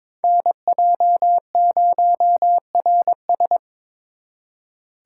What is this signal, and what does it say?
Morse code, heard as a keyed tone, about 25 wpm, 700 Hz. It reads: NJ0RS
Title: Speed Racing - USA 2x2 Call Signs 22wpm